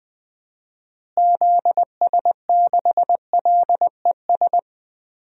Morse code, heard as a keyed tone, about 20 wpm, 700 Hz. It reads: ZS6LES
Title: DXCC-100 Call Signs - 6 Characters - Repeated 3x - 20wpm